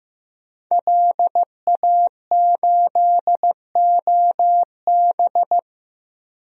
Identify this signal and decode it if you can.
Morse code, heard as a keyed tone, about 15 wpm, 700 Hz. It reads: LA8OB
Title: DXCC-100 Call Signs - 5 Characters - 15wpm